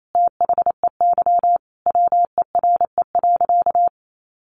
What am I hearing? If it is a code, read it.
Morse code, heard as a keyed tone, about 28 wpm, 700 Hz. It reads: THEY WERE.
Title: Sentences from Top 1000 Words 28wpm